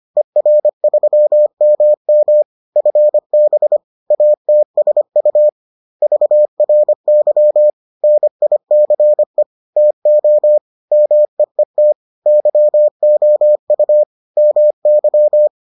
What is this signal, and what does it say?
Morse code, heard as a keyed tone, about 25 wpm, 600 Hz. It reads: ER3MM FB ATSU VRY NICE TO MEET YOU MY